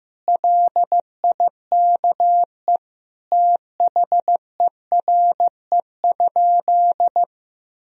Morse code, heard as a keyed tone, about 15 wpm, 700 Hz. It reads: LIKE THERE?
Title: Sentences from Top 1000 Words - Review Part 4 - 15wpm